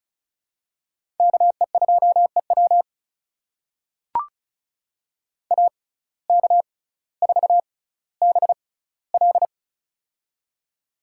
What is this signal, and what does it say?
Morse code, heard as a keyed tone, about 34 wpm, spaced as for 18 wpm, 700 Hz. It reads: KE2EW AK4BL
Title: USA Call Signs - 5 Characters - 6x Character Spacing - 35wpm